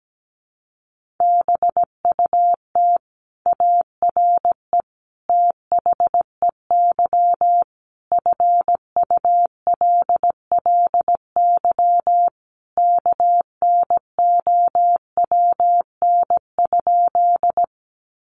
Morse code, heard as a keyed tone, about 17 wpm, 700 Hz. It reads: BUT ARE THEY FULLY KNOWN?
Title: Sentences from Top 1300 Words 17wpm